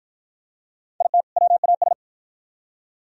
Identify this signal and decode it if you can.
Morse code, heard as a keyed tone, about 52 wpm, 700 Hz. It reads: IT WAS